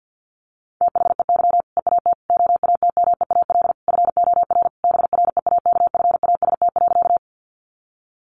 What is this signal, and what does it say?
Morse code, heard as a keyed tone, about 50 wpm, 700 Hz. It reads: THEY EAT OATMEAL FOR BREAKFAST.
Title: Easy Phrases 50wpm